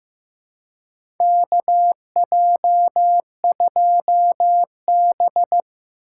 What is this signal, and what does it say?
Morse code, heard as a keyed tone, about 15 wpm, 700 Hz. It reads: KJ2B
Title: USA Call Signs - 4 Characters - 4x Character Spacing - 15wpm